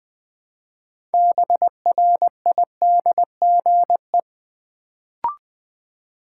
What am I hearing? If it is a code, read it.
Morse code, heard as a keyed tone, about 20 wpm, 700 Hz. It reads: BRIDGE
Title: Build A Word - 3 Letters At A Time - 6-Letter Words - 20wpm